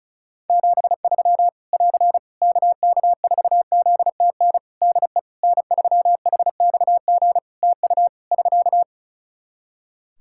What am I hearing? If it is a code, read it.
Morse code, heard as a keyed tone, about 34 wpm, 700 Hz. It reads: 73 <AR> KK4ZTN DE N3HXG TU <SK>
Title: Practice QSO - Exchange 4 - Ending - 35wpm